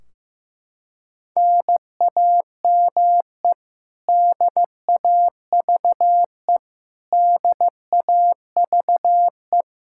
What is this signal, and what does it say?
Morse code, heard as a keyed tone, about 15 wpm, 700 Hz. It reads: NAME DAVE DAVE
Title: Practice QSO - Exchange 1 - Beginning - 15wpm